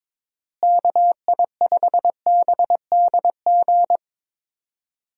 Morse code, heard as a keyed tone, about 22 wpm, 700 Hz. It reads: KI5BDG
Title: USA Call Signs - 6 Characters - Repeated 2x - 22wpm